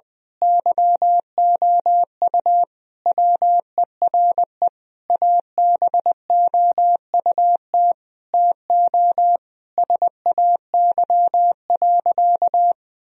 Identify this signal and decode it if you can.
Morse code, heard as a keyed tone, about 21 wpm, 700 Hz. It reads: YOU WERE ABOUT TO SAY.
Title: Sentences from Top 1000 Words - Review Part 2 - 20wpm